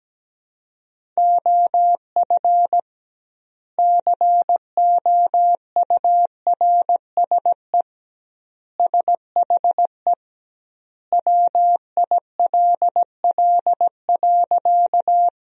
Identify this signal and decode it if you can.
Morse code, heard as a keyed tone, about 17 wpm, 700 Hz. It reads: OF COURSE SHE WILL.
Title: 2x Word-Spacing - Sentences from Top 500 Words - 17wpm